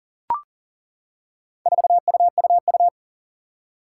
Morse code, heard as a keyed tone, about 37 wpm, 700 Hz. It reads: 4UUU
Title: Four Challenging Letter-Number Combinations 40wpm